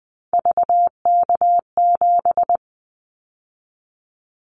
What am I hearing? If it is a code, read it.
Morse code, heard as a keyed tone, about 20 wpm, 700 Hz. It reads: VK7
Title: Callsign Prefixes for Top 50 DXCC - 20wpm